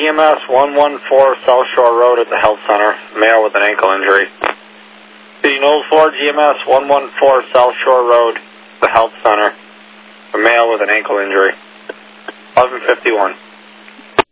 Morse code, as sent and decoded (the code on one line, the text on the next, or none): none